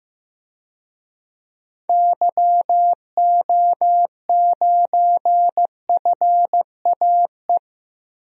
Morse code, heard as a keyed tone, about 15 wpm, 700 Hz. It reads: YO9FAE
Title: DXCC-100 Call Signs - 6 Characters - 15wpm